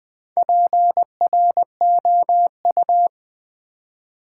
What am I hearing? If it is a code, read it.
Morse code, heard as a keyed tone, about 20 wpm, 700 Hz. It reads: PROU
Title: Build A Word - 1 Letter At A Time - 5-Letter Words - 20wpm